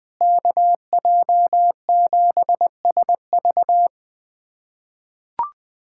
Morse code, heard as a keyed tone, about 20 wpm, 700 Hz. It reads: KJ7SV
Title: USA Callsign DE Callsign - 5 Characters - 20wpm